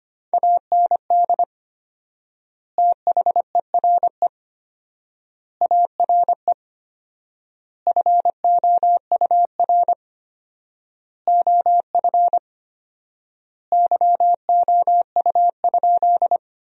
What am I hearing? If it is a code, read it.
Morse code, heard as a keyed tone, about 25 wpm, 700 Hz. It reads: AND THERE ARE FOUR OF YOU?
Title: 4x Word-Spacing - Sentences from Top 300 Words - 25wpm